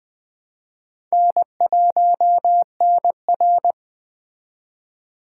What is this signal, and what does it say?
Morse code, heard as a keyed tone, about 24 wpm, 700 Hz. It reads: N1NR
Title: Speed Racing - USA 1x2 Call Signs 20wpm